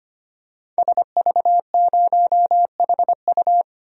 Morse code, heard as a keyed tone, about 25 wpm, 700 Hz. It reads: SV0HU
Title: DXCC-100 Call Signs - 5 Characters - 25wpm